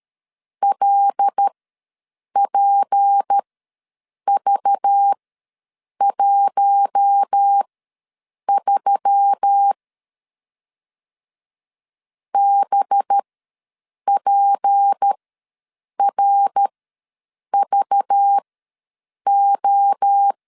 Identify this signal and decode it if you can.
Morse code, heard as a keyed tone, about 13 wpm, spaced as for 7 wpm, 800 Hz. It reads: LPV13 BPRVO
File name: LEZIONI / 6 PKX0 / 6 P ws 12wpm 3117